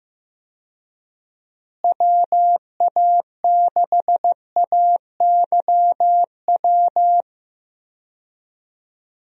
Morse code, heard as a keyed tone, about 15 wpm, 700 Hz. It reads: WA6AYW
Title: USA Call Signs - 6 Characters - Repeated 2x - 15wpm